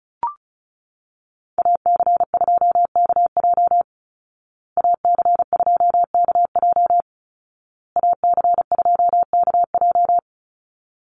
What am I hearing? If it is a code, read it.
Morse code, heard as a keyed tone, about 34 wpm, 700 Hz. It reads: AC2KJ AC2KJ AC2KJ
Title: Call Signs Repeated 2x - 35wpm